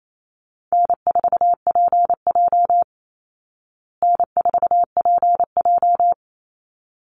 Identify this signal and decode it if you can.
Morse code, heard as a keyed tone, about 28 wpm, 700 Hz. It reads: N4PJ N4PJ
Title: Call Signs Repeated 3x - 28wpm